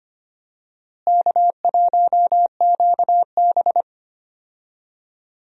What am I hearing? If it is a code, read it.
Morse code, heard as a keyed tone, about 25 wpm, 700 Hz. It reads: K1QB
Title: USA Call Signs - 4 Characters - Repeated 2x - 25wpm